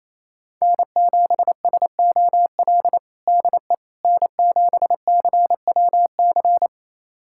Morse code, heard as a keyed tone, about 28 wpm, 700 Hz. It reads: N7SOL DE N7CWC
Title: USA Callsign DE Callsign - 5 Characters - 28wpm